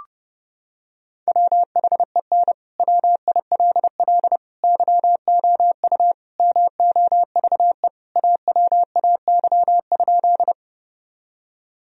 Morse code, heard as a keyed tone, about 32 wpm, 700 Hz. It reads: WHEN WILL YOU MOVE AWAY?
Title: Sentences from Top 1000 Words - Review Part 1 - 30wpm